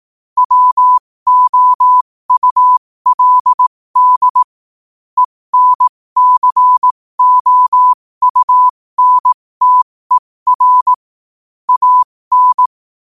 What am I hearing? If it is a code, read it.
Morse code, heard as a keyed tone, about 18 wpm, spaced as for 15 wpm, 1000 Hz. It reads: WOULD ENCOUNTER AN
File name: qotdfiles / 15wpm